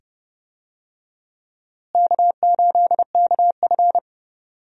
Morse code, heard as a keyed tone, about 30 wpm, 700 Hz. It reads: K8KF
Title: USA Call Signs - 4 Characters - Repeated 2x - 30wpm